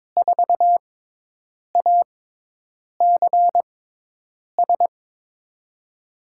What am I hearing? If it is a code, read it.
Morse code, heard as a keyed tone, about 22 wpm, spaced as for 8 wpm, 700 Hz. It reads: 4ACS
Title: USA Call Signs - 6x Character Spacing - 22wpm